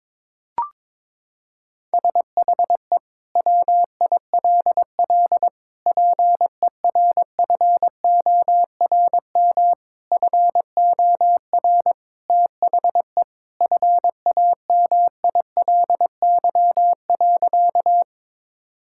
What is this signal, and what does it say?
Morse code, heard as a keyed tone, about 22 wpm, 700 Hz. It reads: SHE WILL PERFORM FOR THE FAMILY.
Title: Sentences from Top 1000 Words - Review Part 5 - 22wpm